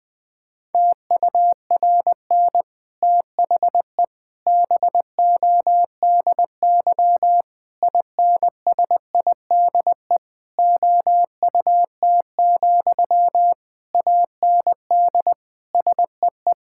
Morse code, heard as a keyed tone, about 21 wpm, 700 Hz. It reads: TURN THE BODY INSIDE OUT, AND SEE
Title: Sentences from Top 1000 Words - Review Part 6 - 20wpm